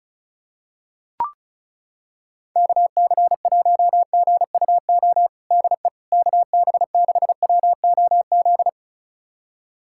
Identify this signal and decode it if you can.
Morse code, heard as a keyed tone, about 36 wpm, 700 Hz. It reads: KC1GUO DE KB6WOZ
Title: USA Callsign DE Callsign 35wpm